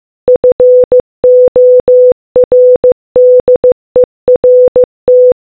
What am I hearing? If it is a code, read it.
Morse code, heard as a keyed tone, about 15 wpm, 500 Hz. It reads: FORDERT